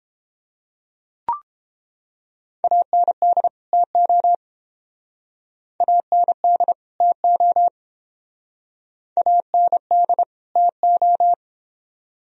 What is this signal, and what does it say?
Morse code, heard as a keyed tone, about 29 wpm, 700 Hz. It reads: AND TO AND TO AND TO